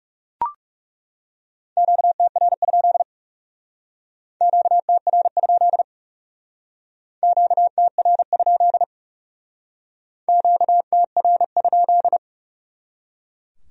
Morse code, heard as a keyed tone, about 37 wpm, 700 Hz. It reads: QTR? QTR? QTR? QTR?
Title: Speed-Racing Q-Codes - 30wpm